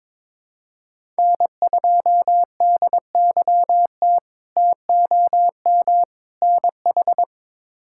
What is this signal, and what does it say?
Morse code, heard as a keyed tone, about 22 wpm, 700 Hz. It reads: N2DYT TOM NH
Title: K1USN SST QSO Practice - 22wpm